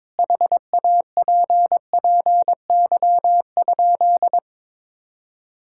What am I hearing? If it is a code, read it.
Morse code, heard as a keyed tone, about 22 wpm, 700 Hz. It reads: HAPPY?
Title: Sentences from Top 1000 Words - Review Part 5 - 22wpm